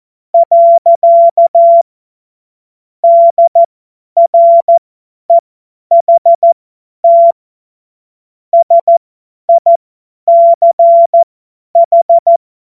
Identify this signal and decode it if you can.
Morse code, heard as a keyed tone, about 14 wpm, spaced as for 10 wpm, 680 Hz. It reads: . DREHT SICH
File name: Witz-09-September-2025-50-F70